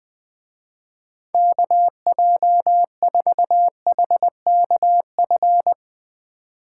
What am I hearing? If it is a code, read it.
Morse code, heard as a keyed tone, about 20 wpm, 700 Hz. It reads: KJ4HKF
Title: Speed Racing - USA 2x3 Call Signs 20wpm